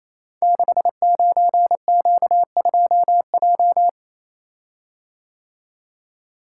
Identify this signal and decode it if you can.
Morse code, heard as a keyed tone, about 29 wpm, 700 Hz. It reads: 69Q2J
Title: Code Groups - 5 Letter:Numbers - 28wpm